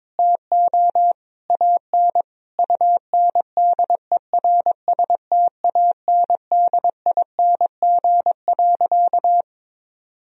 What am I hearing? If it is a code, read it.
Morse code, heard as a keyed tone, about 22 wpm, 700 Hz. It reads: TO AN UNDERSTANDING.
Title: Sentences from Top 1000 Words - Review Part 3 - 22wpm